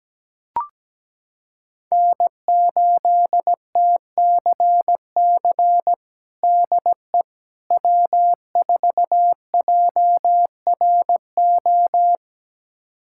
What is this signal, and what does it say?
Morse code, heard as a keyed tone, about 17 wpm, 700 Hz. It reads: N8TCC DE W4JRO
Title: USA Callsign DE Callsign - 5 Characters - 17wpm